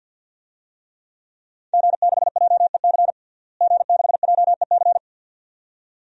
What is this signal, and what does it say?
Morse code, heard as a keyed tone, about 51 wpm, 700 Hz. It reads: G6JEC G6JEC
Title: DXCC-100 Call Signs - 5 Characters - Repeated 2x - 50wpm